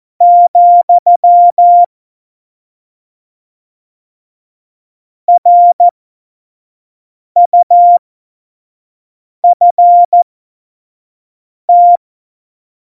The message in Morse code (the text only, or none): , RUFT